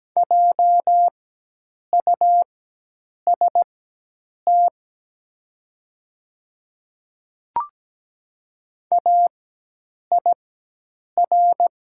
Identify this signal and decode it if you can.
Morse code, heard as a keyed tone, about 17 wpm, spaced as for 8 wpm, 700 Hz. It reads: JUST AIR
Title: 4x Character Spacing - Top 300 Words - 17wpm